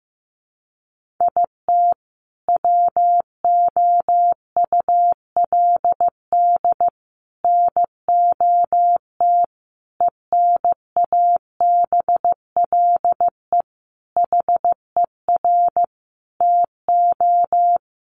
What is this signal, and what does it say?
Morse code, heard as a keyed tone, about 15 wpm, 700 Hz. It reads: IT WOULD NOT ENABLE HER TO
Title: Sentences from Top 1500 Words 15wpm